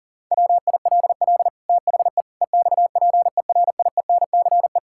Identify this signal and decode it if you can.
Morse code, heard as a keyed tone, about 37 wpm, 700 Hz. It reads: WILL THE EXPERIENCE
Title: Sentences from Top 1000 Words - Review Part 5 - 40wpm